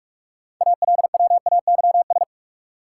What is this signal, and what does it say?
Morse code, heard as a keyed tone, about 46 wpm, 700 Hz. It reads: ALWAYS
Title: Sentences from Top 300 Words - New Word Drill - 45wpm